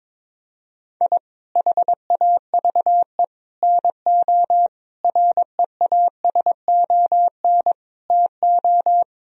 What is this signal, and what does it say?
Morse code, heard as a keyed tone, about 22 wpm, 700 Hz. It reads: I HAVE NO REASON TO
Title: Sentences from Top 1000 Words - Review Part 1 - 22wpm